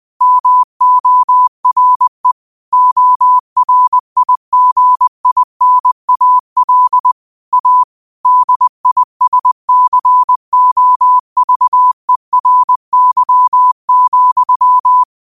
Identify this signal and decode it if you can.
Morse code, heard as a keyed tone, about 20 wpm, 1000 Hz. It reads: MORE ORIGINAL A DISCOVERY,
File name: qotdfiles / 20wpm